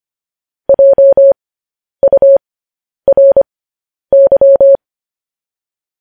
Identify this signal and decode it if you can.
Morse code, heard as a keyed tone, about 25 wpm, spaced as for 10 wpm, 550 Hz. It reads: JURY
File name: podcast / News headlines-1x-25x10